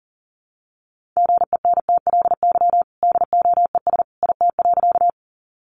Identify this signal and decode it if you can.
Morse code, heard as a keyed tone, about 40 wpm, 700 Hz. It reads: GENTLY DOES IT.
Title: Sentences from Top 1500 Words 40wpm